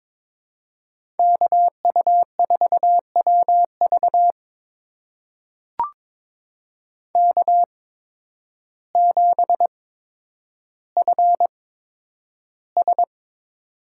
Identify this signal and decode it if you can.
Morse code, heard as a keyed tone, about 22 wpm, spaced as for 10 wpm, 700 Hz. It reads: KU4WV K7FS
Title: USA Call Signs - 5 Characters - 8x Character Spacing - 22wpm